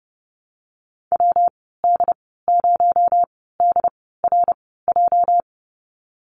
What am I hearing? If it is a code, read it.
Morse code, heard as a keyed tone, about 30 wpm, spaced as for 17 wpm, 700 Hz. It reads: WD0DRJ
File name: Lesson 37g - Letter x - Call Signs 30wpm